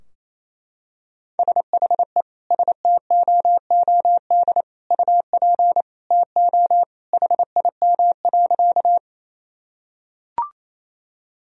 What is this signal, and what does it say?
Morse code, heard as a keyed tone, about 28 wpm, 700 Hz. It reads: SHE STOOD UP TO HIM.
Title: Sentences from Top 1000 Words - Review Part 2 - 28wpm